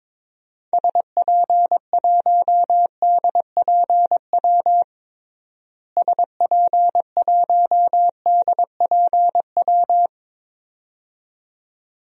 Morse code, heard as a keyed tone, about 22 wpm, 700 Hz. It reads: SP1DPW SP1DPW
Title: DXCC-100 Call Signs - 6 Characters - Repeated 2x - 22wpm